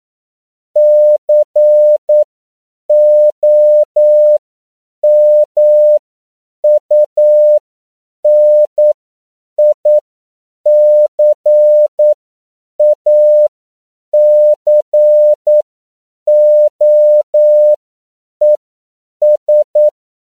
Morse code, heard as a keyed tone, about 9 wpm, spaced as for 7 wpm, 600 Hz. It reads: COMUNICACOES
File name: CW / 10PPM